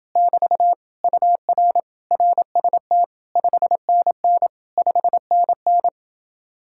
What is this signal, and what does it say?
Morse code, heard as a keyed tone, about 30 wpm, 700 Hz. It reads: <BT> UR RST 5NN 5NN